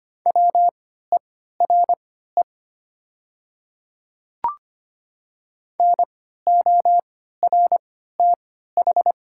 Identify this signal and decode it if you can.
Morse code, heard as a keyed tone, about 25 wpm, spaced as for 14 wpm, 700 Hz. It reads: WERE NORTH
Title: 3x Character Spacing - Top 300 Words - 25wpm